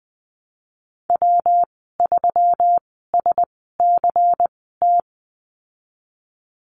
Morse code, heard as a keyed tone, about 20 wpm, spaced as for 15 wpm, 700 Hz. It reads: W3SCT
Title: 173 Lesson 28g - Number 9 - Call Signs 20wpm